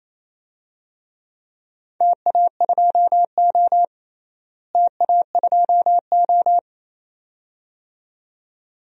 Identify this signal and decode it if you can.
Morse code, heard as a keyed tone, about 27 wpm, 700 Hz. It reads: TA2O TA2O
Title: DXCC-100 Call Signs - 4 Characters - Repeated 2x - 28wpm